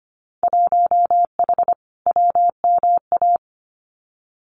Morse code, heard as a keyed tone, about 24 wpm, 700 Hz. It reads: 1H WMA
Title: Winter Field Day Exchanges Encore 25wpm